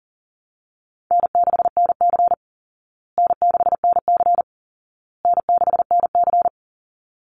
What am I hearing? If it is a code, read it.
Morse code, heard as a keyed tone, about 40 wpm, 700 Hz. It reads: N6NC N6NC N6NC
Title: Easy Call Signs Repeated 3x - 40wpm